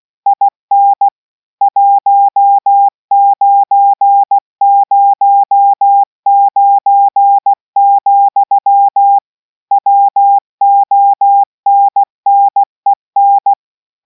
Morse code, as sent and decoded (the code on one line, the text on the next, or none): .. -. / .---- ----. ----- ----. --..-- / .-- --- -. -. -.
IN 1909, WONNEN